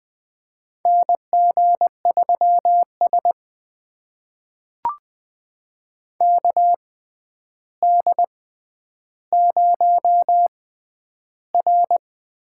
Morse code, heard as a keyed tone, about 20 wpm, spaced as for 10 wpm, 700 Hz. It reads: NG3S KD0R
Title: USA Call Signs - 6x Character Spacing - 20wpm